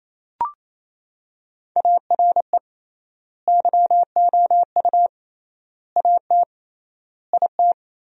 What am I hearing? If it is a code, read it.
Morse code, heard as a keyed tone, about 28 wpm, 700 Hz. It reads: ARE YOU AT IT
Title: 3x Word-Spacing - Sentences from Top 200 Words - 28wpm